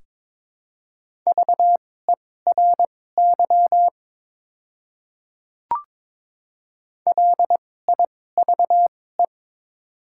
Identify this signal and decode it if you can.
Morse code, heard as a keyed tone, about 21 wpm, spaced as for 16 wpm, 700 Hz. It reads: VERY LIVE
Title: 2x Character Spacing - Top 200 Words - 22wpm